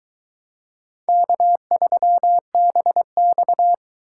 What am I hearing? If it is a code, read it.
Morse code, heard as a keyed tone, about 22 wpm, 700 Hz. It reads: K3BX